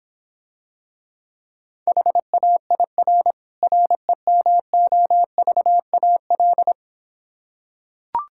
Morse code, heard as a keyed tone, about 26 wpm, 700 Hz. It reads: HAIR REMOVAL